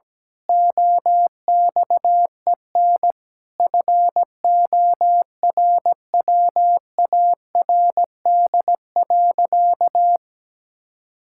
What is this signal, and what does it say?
Morse code, heard as a keyed tone, about 17 wpm, 700 Hz. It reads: OXEN FORWARD.
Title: Pangram Challenge v2 - 17wpm